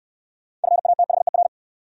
Morse code, heard as a keyed tone, about 68 wpm, 690 Hz. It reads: FATHER